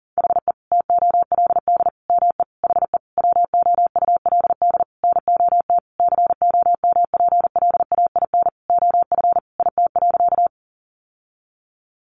Morse code, heard as a keyed tone, about 40 wpm, 700 Hz. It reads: HE TOLD ME HE WOULD NOT COMPLAIN OF IT.
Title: Sentences from Top 1300 Words 40wpm